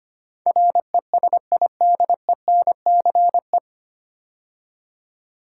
Morse code, heard as a keyed tone, about 25 wpm, 700 Hz. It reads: RESIDENCE
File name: practice / 25wpm / 9 Letter Words - First 3 Letters Given - 25wpm